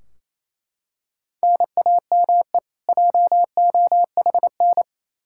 Morse code, heard as a keyed tone, about 28 wpm, 700 Hz. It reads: NAME JOHN
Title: Practice QSO - Exchange 1 - Beginning - 28wpm